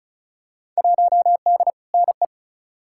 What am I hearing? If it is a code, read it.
Morse code, heard as a keyed tone, about 36 wpm, 700 Hz. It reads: 1D NE
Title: Field Day Exchanges - 35wpm